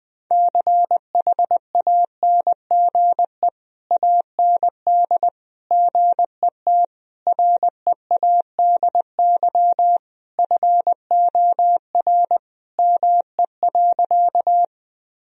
Morse code, heard as a keyed tone, about 20 wpm, 700 Hz. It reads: CHANGE AND GET READY FOR ME.
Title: Sentences from Top 1000 Words - Review Part 1 - 20wpm